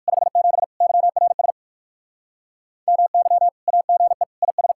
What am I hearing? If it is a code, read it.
Morse code, heard as a keyed tone, about 61 wpm, 700 Hz. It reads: 57 YRS MY AGE IS